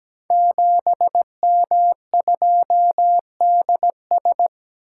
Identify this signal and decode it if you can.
Morse code, heard as a keyed tone, about 17 wpm, 700 Hz. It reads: 7M2DS
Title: Code Groups - 5 Letter:Numbers - 17wpm